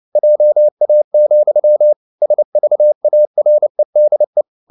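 Morse code, heard as a keyed tone, about 29 wpm, 600 Hz. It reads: JA, SVAREDE
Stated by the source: Fast (QRQ) Morse at 29 wpm, from CW med Gnister.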